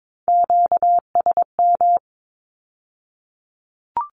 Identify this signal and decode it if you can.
Morse code, heard as a keyed tone, about 22 wpm, 700 Hz. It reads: QSM
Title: Q-Codes with Spoken Meaning - 22wpm